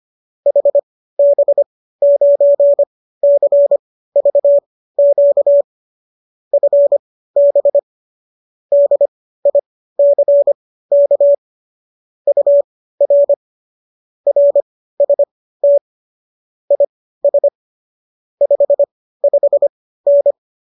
Morse code, heard as a keyed tone, about 25 wpm, spaced as for 15 wpm, 570 Hz. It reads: HB9CVQ FB DICK UR RST IS 55N